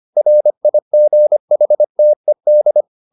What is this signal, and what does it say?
Morse code, heard as a keyed tone, about 25 wpm, 600 Hz. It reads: RIGHTED
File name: the old man and the sea part 1 25wpm